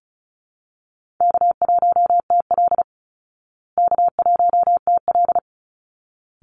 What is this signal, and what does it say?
Morse code, heard as a keyed tone, about 34 wpm, 700 Hz. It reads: K1TL K1TL
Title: Easy Call Signs Repeated 2x - 35wpm